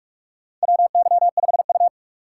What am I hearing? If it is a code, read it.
Morse code, heard as a keyed tone, about 46 wpm, 700 Hz. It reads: WY5U
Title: USA Call Signs - 2x Character Spacing - 45wpm